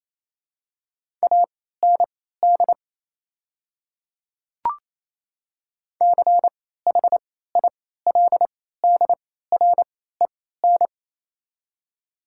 Morse code, heard as a keyed tone, about 27 wpm, spaced as for 16 wpm, 700 Hz. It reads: AND CHILDREN
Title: 3x Character Spacing - Top 300 Words - 28wpm